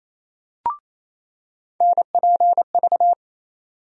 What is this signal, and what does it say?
Morse code, heard as a keyed tone, about 28 wpm, 700 Hz. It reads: NPV
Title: USA Call Sign Postfixes - 3 Letters 28wpm